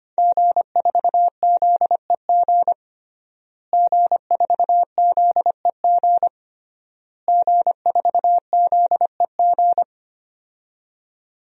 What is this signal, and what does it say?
Morse code, heard as a keyed tone, about 25 wpm, 700 Hz. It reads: G4ZEG G4ZEG G4ZEG
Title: DXCC-100 Call Signs - 5 Characters - Repeated 3x - 25wpm